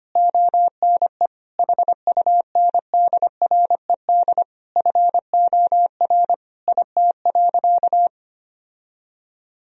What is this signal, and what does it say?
Morse code, heard as a keyed tone, about 25 wpm, 700 Hz. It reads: ONE HUNDRED FOR IT.
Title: Sentences from Top 1000 Words - Review Part 2 - 25wpm